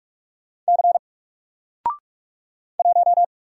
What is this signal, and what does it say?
Morse code, heard as a keyed tone, about 46 wpm, 700 Hz. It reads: C 1
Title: Single Letter-Number 45wpm